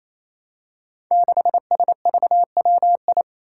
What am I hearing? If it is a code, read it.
Morse code, heard as a keyed tone, about 28 wpm, 700 Hz. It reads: 6SVWI
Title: Five Challenging Letter-Number Combinations 28wpm